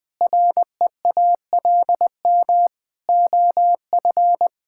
Morse code, heard as a keyed tone, about 20 wpm, 700 Hz. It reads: REALM OF
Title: Sentences with 9-Word Phrases - Spoken Beginning - 20wpm